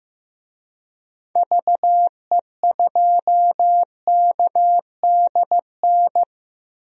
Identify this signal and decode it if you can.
Morse code, heard as a keyed tone, about 15 wpm, 700 Hz. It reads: VE2KDN
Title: DXCC-100 Call Signs - 6 Characters - Repeated 3x - 15wpm